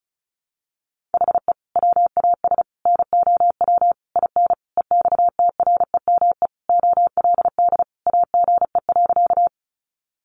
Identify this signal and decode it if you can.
Morse code, heard as a keyed tone, about 34 wpm, 700 Hz. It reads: HE WAS NOW IN EXTREME OLD AGE.
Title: Sentences from Top 1300 Words 35wpm